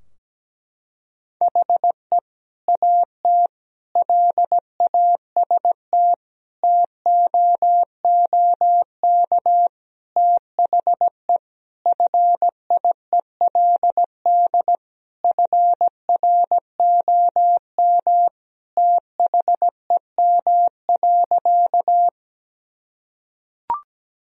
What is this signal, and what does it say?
Morse code, heard as a keyed tone, about 17 wpm, 700 Hz. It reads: HE AT LAST TOOK THE FIELD FROM THEM.
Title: Sentences from Top 1000 Words - Review Part 2 - 17wpm